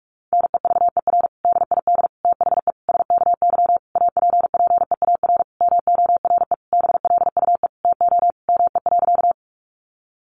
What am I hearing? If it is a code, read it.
Morse code, heard as a keyed tone, about 44 wpm, 700 Hz. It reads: NEVER DID THE SKY APPEAR MORE BLUE TO ME.
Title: Sentences from Top 1200 Words 45wpm